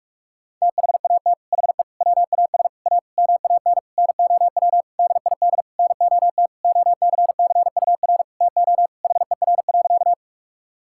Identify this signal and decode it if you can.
Morse code, heard as a keyed tone, about 46 wpm, 700 Hz. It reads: THAT HE WAS A MAN NOW DID NOT OCCUR TO HER.
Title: Sentences from Top 1000 Words - Review Part 6 - 45wpm